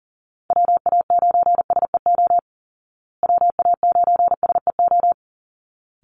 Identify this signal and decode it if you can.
Morse code, heard as a keyed tone, about 40 wpm, 700 Hz. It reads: WA9SEO WA9SEO
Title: Call Signs Repeated 3x - 40wpm